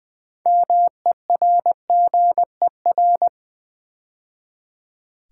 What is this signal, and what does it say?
Morse code, heard as a keyed tone, about 20 wpm, 700 Hz. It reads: MERGER
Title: 6 Letter Words - First 3 Letters Given - 20wpm